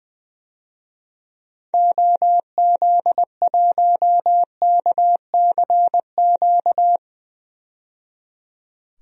Morse code, heard as a keyed tone, about 20 wpm, 700 Hz. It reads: OZ1KCQ
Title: DXCC-100 Call Signs - 6 Characters - 20wpm